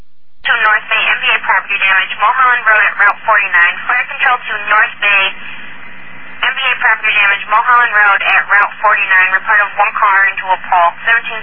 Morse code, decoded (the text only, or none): none